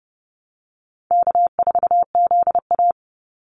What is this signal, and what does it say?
Morse code, heard as a keyed tone, about 30 wpm, 700 Hz. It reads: K4ZA
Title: Easy Call Signs Repeated 3x - 30wpm